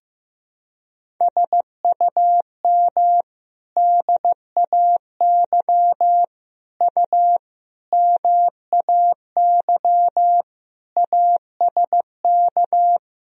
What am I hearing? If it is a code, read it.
Morse code, heard as a keyed tone, about 15 wpm, 700 Hz. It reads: SUM DAY U MAY ASK
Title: Sentences from Top 200 Words and Abbreviations 15wpm